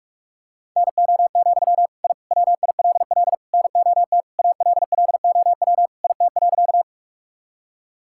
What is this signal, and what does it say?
Morse code, heard as a keyed tone, about 46 wpm, 700 Hz. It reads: NO, I WILL NOT ALLOW IT.
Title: Sentences from Top 1000 Words - Review Part 5 - 45wpm